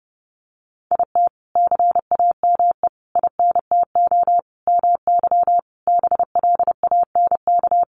Morse code, heard as a keyed tone, about 28 wpm, 700 Hz. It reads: IT CAME INTO MY BLANK
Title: Sentences from Top 1500 Words 30wpm